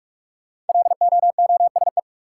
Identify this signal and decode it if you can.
Morse code, heard as a keyed tone, about 46 wpm, 700 Hz. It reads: LOOSE
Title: Five Letter Words - 45wpm